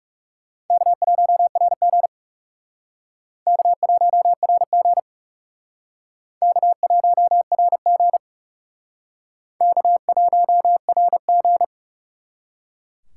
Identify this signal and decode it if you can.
Morse code, heard as a keyed tone, about 38 wpm, 700 Hz. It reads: K1RG K1RG K1RG K1RG